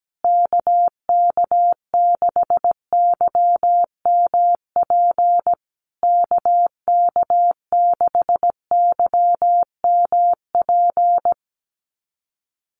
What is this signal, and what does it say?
Morse code, heard as a keyed tone, about 17 wpm, 700 Hz. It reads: KK6YMP KK6YMP
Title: USA Call Signs Repeated 3x 17wpm